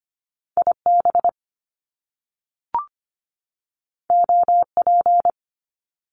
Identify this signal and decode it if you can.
Morse code, heard as a keyed tone, about 25 wpm, 700 Hz. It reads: IB OP
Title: Top Two Letter Combinations in Words 25wpm